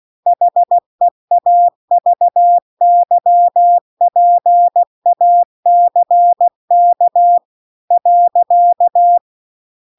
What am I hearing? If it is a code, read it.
Morse code, heard as a keyed tone, about 16 wpm, 700 Hz. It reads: HEAVY PACK .